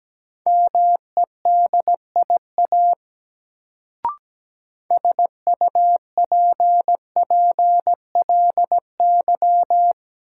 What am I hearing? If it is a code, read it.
Morse code, heard as a keyed tone, about 17 wpm, 700 Hz. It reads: MEDIA SUPPLY
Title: Sentences from Top 1000 Words - Review Part 4 - New Word Drill - 17wpm